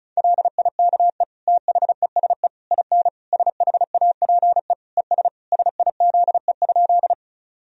Morse code, heard as a keyed tone, about 36 wpm, 700 Hz. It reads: LIKE THESE IN SHAPE ES SIZE?
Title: Sentences from Top 500 Words and Abbreviations 35wpm